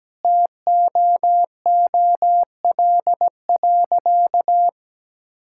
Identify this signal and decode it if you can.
Morse code, heard as a keyed tone, about 17 wpm, 700 Hz. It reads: TOOL.
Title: Sentences from Top 1000 Words - Review Part 6 - 17wpm